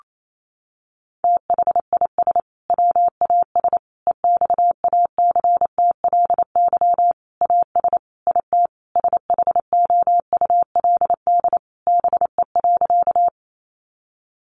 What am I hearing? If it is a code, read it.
Morse code, heard as a keyed tone, about 28 wpm, 700 Hz. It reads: THIS WAS EXACTLY AS IT SHOULD BE.
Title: Sentences from Top 900 Words 28wpm